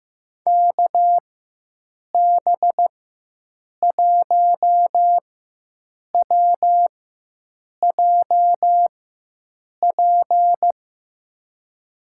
Morse code, heard as a keyed tone, about 15 wpm, spaced as for 7 wpm, 700 Hz. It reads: KB1WJP
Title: USA Call Signs - 6 Characters - 4x Character Spacing - 15wpm